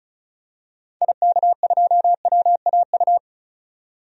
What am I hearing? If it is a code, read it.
Morse code, heard as a keyed tone, about 36 wpm, 700 Hz. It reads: IK2WAU
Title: DXCC-100 Call Signs - 6 Characters - 35wpm